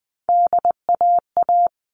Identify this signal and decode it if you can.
Morse code, heard as a keyed tone, about 20 wpm, 700 Hz. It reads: DAA
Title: All Cut Numbers - 3-Digit 20wpm